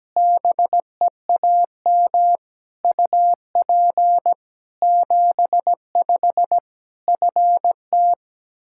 Morse code, heard as a keyed tone, about 17 wpm, 700 Hz. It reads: BEAM UP 75 FT
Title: Practice QSO - Exchange 2 - Info - 17wpm